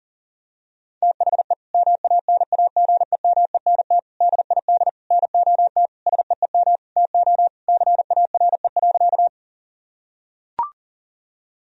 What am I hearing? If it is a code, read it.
Morse code, heard as a keyed tone, about 40 wpm, 700 Hz. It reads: THE MANAGEMENT DID NOT SEEM TO CARE.
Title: Sentences from Top 1000 Words - Review Part 4 - 40wpm